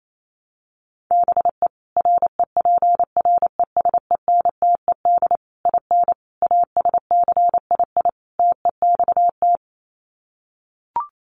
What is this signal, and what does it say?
Morse code, heard as a keyed tone, about 28 wpm, 700 Hz. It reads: BE REPRESENTED IN ASCII TEXT
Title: Sets of 5 Words 28wpm